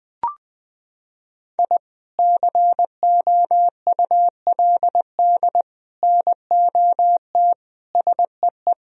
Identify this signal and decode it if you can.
Morse code, heard as a keyed tone, about 21 wpm, 700 Hz. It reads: I COULD NOT SEE
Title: Sentences from Top 1000 Words - Review Part 4 - 20wpm